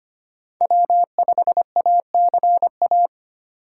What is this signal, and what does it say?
Morse code, heard as a keyed tone, about 25 wpm, 700 Hz. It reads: W5ACA
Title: USA Call Signs - 3x Character Spacing - 25wpm